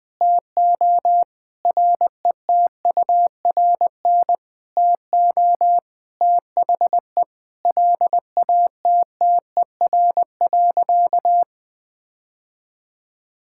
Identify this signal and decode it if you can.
Morse code, heard as a keyed tone, about 20 wpm, 700 Hz. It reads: TO RETURN TO THE LATTER.
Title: Sentences from Top 1600 Words 20wpm